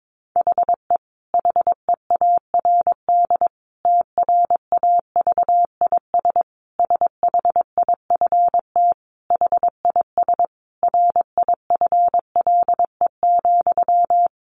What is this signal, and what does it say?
Morse code, heard as a keyed tone, about 22 wpm, 700 Hz. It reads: HE HEARD TRAVIS SHIFT HIS RIFLE,
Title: A Sound Of Thunder 22wpm